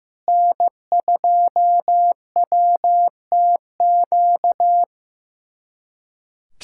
Morse code, heard as a keyed tone, about 15 wpm, 700 Hz. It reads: N2WTQ
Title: USA Callsign DE Callsign - 5 Characters - 15wpm